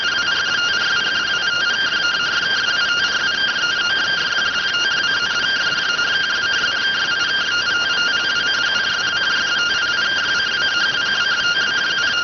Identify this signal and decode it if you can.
Two-tone teleprinter signal, about 170 Hz shift.